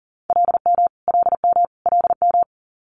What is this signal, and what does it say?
Morse code, heard as a keyed tone, about 42 wpm, 700 Hz. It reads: LM LM LM
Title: Multi-letter Phonemes Repeated 3x 40wpm